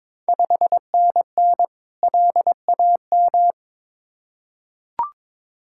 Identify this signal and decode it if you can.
Morse code, heard as a keyed tone, about 22 wpm, 690 Hz. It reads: 5NN LAM
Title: Ontario QSO Party Exchange 22wpm